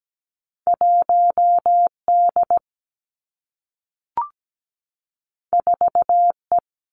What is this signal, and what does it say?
Morse code, heard as a keyed tone, about 17 wpm, 700 Hz. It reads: 1 D 4 E
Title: Two Challenging Letter-Number Combinations 17wpm